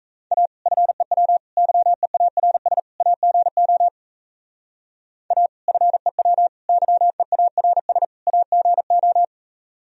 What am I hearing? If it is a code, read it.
Morse code, heard as a keyed tone, about 36 wpm, 700 Hz. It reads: A FEW YEARS AGO A FEW YEARS AGO